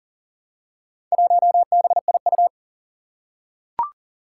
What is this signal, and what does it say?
Morse code, heard as a keyed tone, about 40 wpm, 700 Hz. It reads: 1BIU
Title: Four Challenging Letter-Number Combinations 40wpm